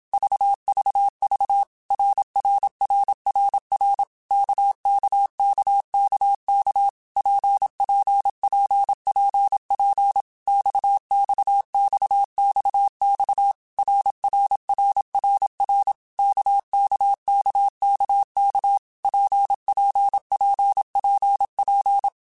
Morse code, heard as a keyed tone, about 27 wpm, 800 Hz. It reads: VVV RRRRR KKKKK PPPPP XXXXX RRRRR KKKKK PPPPP
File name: Basic Letter Groups By7Groups / RKPX / RKPX@25